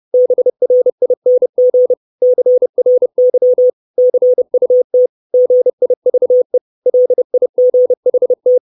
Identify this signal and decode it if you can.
Morse code, heard as a keyed tone, about 30 wpm, 500 Hz. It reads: BRING CRY CUT GIVE LIGHT
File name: 632 Words 30WPM